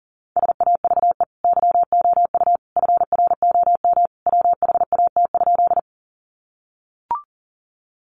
Morse code, heard as a keyed tone, about 40 wpm, 700 Hz. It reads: SAVE YOU FROM WHAT?
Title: Sentences from Top 1100 Words 40wpm